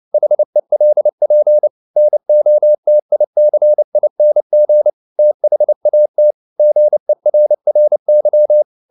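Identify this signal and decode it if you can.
Morse code, heard as a keyed tone, about 29 wpm, 600 Hz. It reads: HELP NOTICING THAT GERRY
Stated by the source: Morse code at 29 wpm, sending Ken Follett's novel Never in English, from CW med Gnister.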